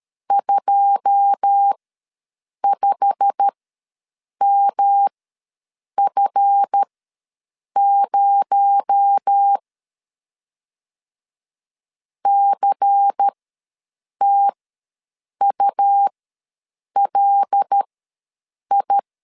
Morse code, heard as a keyed tone, about 13 wpm, spaced as for 7 wpm, 800 Hz. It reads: 25MF0 CTULI
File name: LEZIONI / 9 2 ws 12wpm 3406